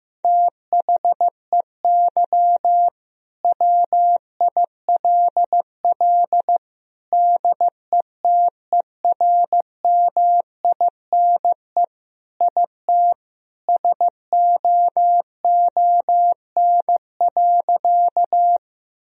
Morse code, heard as a keyed tone, about 15 wpm, 700 Hz. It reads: THEY WILL DETERMINE IT SOON.
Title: Sentences from Top 1000 Words - Review Part 4 - 15wpm